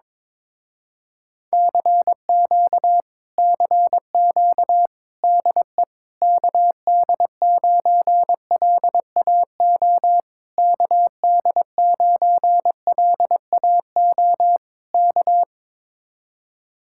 Morse code, heard as a keyed tone, about 21 wpm, 700 Hz. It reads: CQ CQ DE KD9LAO KD9LAO K
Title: Speed-Racing Common QSO Phrases Encore - 22wpm